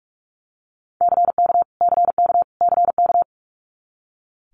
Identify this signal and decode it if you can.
Morse code, heard as a keyed tone, about 46 wpm, 700 Hz. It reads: CK CK CK
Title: Multi-letter Phonemes Repeated 3x 45wpm